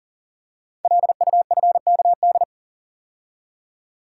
Morse code, heard as a keyed tone, about 39 wpm, 700 Hz. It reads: LUFKD
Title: Code Groups - 5 Letters - 40wpm